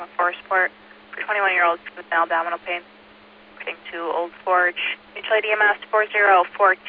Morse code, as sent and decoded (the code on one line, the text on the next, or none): none